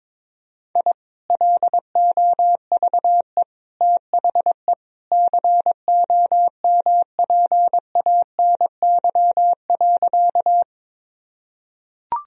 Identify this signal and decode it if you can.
Morse code, heard as a keyed tone, about 22 wpm, 700 Hz. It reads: I LOVE THE COMPANY.
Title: Sentences from Top 1000 Words - Review Part 3 - 22wpm